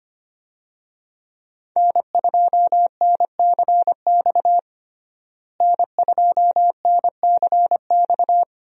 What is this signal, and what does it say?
Morse code, heard as a keyed tone, about 25 wpm, 700 Hz. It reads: N2NCX N2NCX
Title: USA Call Signs - 5 Characters - Repeated 2x - 25wpm